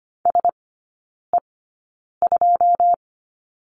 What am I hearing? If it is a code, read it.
Morse code, heard as a keyed tone, about 25 wpm, spaced as for 9 wpm, 700 Hz. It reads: SE2
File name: Lesson 14e - Number 2 - Three Characters 25wpm